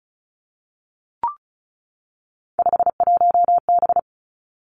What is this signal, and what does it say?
Morse code, heard as a keyed tone, about 36 wpm, 700 Hz. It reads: 51B
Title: Three Challenging Letter-Number Combinations 35wpm